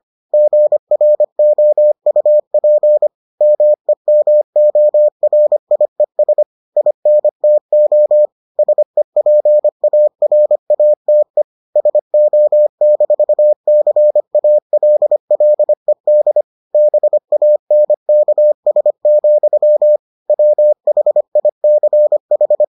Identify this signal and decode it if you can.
Morse code, heard as a keyed tone, about 25 wpm, 600 Hz. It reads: GROUP MEMORIES INTO SEPARATE SO-CALLED BANKS, WHICH